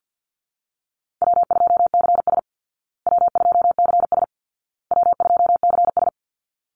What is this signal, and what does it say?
Morse code, heard as a keyed tone, about 50 wpm, 700 Hz. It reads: W2CS W2CS W2CS
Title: Easy Call Signs Repeated 3x - 50wpm